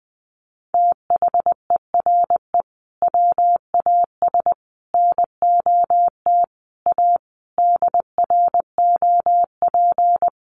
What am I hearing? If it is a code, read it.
Morse code, heard as a keyed tone, about 20 wpm, 700 Hz. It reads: THERE WAS NOT A DROP
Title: Sentences from Top 1500 Words 20wpm